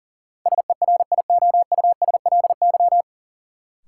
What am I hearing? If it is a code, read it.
Morse code, heard as a keyed tone, about 40 wpm, 700 Hz. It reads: SERIOUSLY
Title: Sentences from Top 1200 Words - New Word Drill - 40wpm